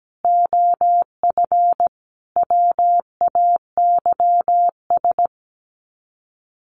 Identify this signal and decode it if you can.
Morse code, heard as a keyed tone, about 17 wpm, 700 Hz. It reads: OF WAYS
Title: Sets of 5 Words Encore 17wpm